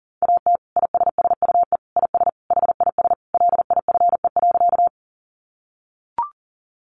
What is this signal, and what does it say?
Morse code, heard as a keyed tone, about 40 wpm, 700 Hz. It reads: AT ISSUE IS HIS LIFE.
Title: Sentences from Top 900 Words 40wpm